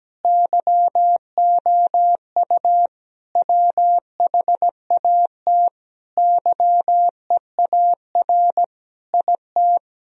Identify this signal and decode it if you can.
Morse code, heard as a keyed tone, about 17 wpm, 700 Hz. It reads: YOU WHAT YEAR IT
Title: Sentences from Top 1000 Words - Review Part 1 - 17wpm